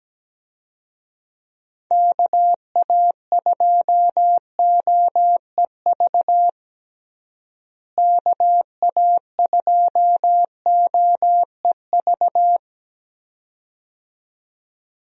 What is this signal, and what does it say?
Morse code, heard as a keyed tone, about 17 wpm, 700 Hz. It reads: KA2OEV KA2OEV
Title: USA Call Signs - 6 Characters - Repeated 2x - 17wpm